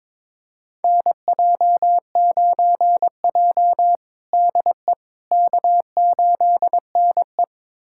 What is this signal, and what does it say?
Morse code, heard as a keyed tone, about 22 wpm, 700 Hz. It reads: NJ9J DE K8NE
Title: USA Callsign DE Callsign - 4 Characters - 22wpm